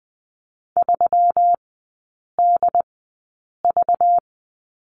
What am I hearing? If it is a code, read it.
Morse code, heard as a keyed tone, about 20 wpm, spaced as for 8 wpm, 700 Hz. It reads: 3DV
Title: 191 Lesson 31e - Letter v - Three Characters 20wpm